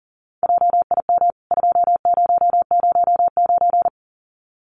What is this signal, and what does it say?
Morse code, heard as a keyed tone, about 40 wpm, 700 Hz. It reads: JIM 2009
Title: CWT Exchange 40wpm